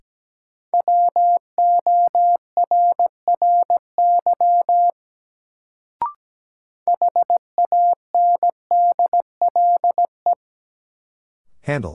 Morse code, heard as a keyed tone, about 17 wpm, 700 Hz. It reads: WORRY HANDLE
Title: Sentences from Top 1100 Words - New Word Drill Alt - 17wpm